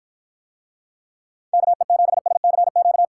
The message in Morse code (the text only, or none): KE7SCQ